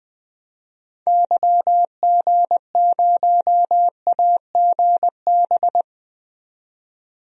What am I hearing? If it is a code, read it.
Morse code, heard as a keyed tone, about 20 wpm, 700 Hz. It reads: YG0AGB
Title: DXCC-100 Call Signs - 6 Characters - 20wpm